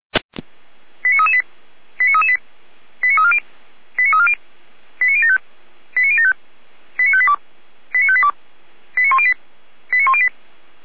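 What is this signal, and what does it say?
Five-tone selective call (ZVEI), heard as quick runs of five tones, tones of about 70 ms. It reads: 89338 89338 89340 89340 89075 89075 89582 89582 89108 89108
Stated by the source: ZVEI five-tone selective call received by SDR.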